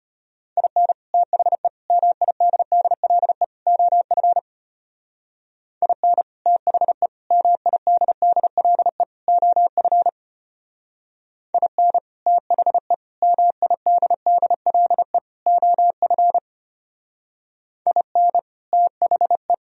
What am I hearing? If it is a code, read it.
Morse code, heard as a keyed tone, about 31 wpm, 700 Hz. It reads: IN THE MIDDLE OF IN THE MIDDLE OF IN THE MIDDLE OF IN THE
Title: Speed-Racing - Sets of 4 Words 25wpm